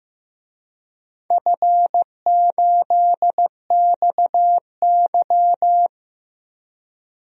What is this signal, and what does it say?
Morse code, heard as a keyed tone, about 15 wpm, 700 Hz. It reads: F8XY
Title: DXCC-100 Call Signs - 4 Characters - Repeated 2x - 15wpm